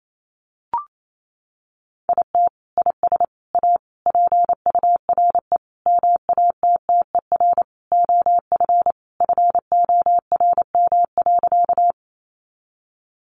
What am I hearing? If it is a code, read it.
Morse code, heard as a keyed tone, about 28 wpm, 700 Hz. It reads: IT IS A PURE MATTER OF FORM.
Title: Sentences from Top 1500 Words 28wpm